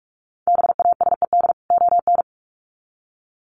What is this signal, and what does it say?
Morse code, heard as a keyed tone, about 44 wpm, 700 Hz. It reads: BASED ON
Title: Sets of 2 Words 45wpm